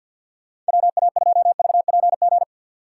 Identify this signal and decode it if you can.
Morse code, heard as a keyed tone, about 59 wpm, 700 Hz. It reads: WA2VPG
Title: Speed Racing - USA 2x3 Call Signs 50wpm